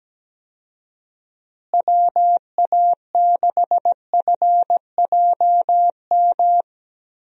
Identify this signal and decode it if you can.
Morse code, heard as a keyed tone, about 17 wpm, 700 Hz. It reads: WA6FJM
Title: USA Call Signs - 6 Characters - Repeated 2x - 17wpm